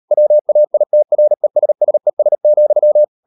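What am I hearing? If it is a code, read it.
Morse code, heard as a keyed tone, about 38 wpm, 600 Hz. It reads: WAITRESSES,
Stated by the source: Morse code practice at 38 WPM, sending Ken Follett's novel Never in English, from CW med Gnister.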